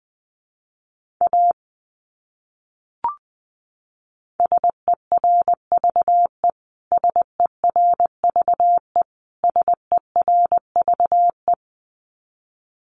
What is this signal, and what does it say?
Morse code, heard as a keyed tone, about 20 wpm, 700 Hz. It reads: A SERVE SERVE SERVE
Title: Top 500 Words Repeated 20wpm